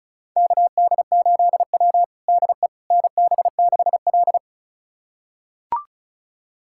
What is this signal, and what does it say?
Morse code, heard as a keyed tone, about 36 wpm, 700 Hz. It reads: KD8W DE NB6L
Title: USA Callsign DE Callsign - 4 Characters - 35wpm